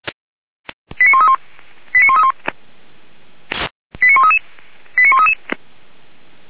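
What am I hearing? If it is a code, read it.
Five-tone selective call (ZVEI), heard as quick runs of five tones, tones of about 70 ms. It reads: 89131 89131 89133 89133
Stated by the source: Five-tone ZVEI selective call 89200, received on SDR, from the feuerwehr-orlamuende.org fire brigade audio server.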